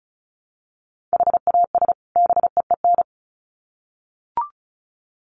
Morse code, heard as a keyed tone, about 34 wpm, 700 Hz. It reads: HAS BEEN
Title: Sets of 2 Words 35wpm